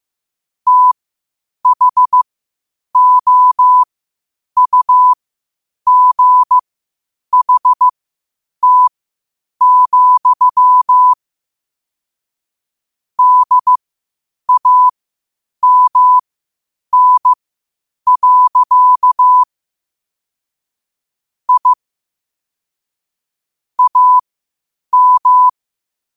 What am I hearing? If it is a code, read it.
Morse code, heard as a keyed tone, about 15 wpm, spaced as for 8 wpm, 1000 Hz. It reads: THOUGHT, DAMN. I AM
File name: qotdfiles / 8wpm